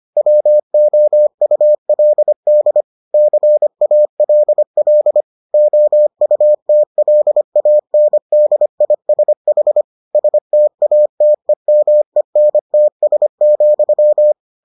Morse code, heard as a keyed tone, about 25 wpm, 600 Hz. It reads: WOULD CALL OUTLANDISH STATEMENTS,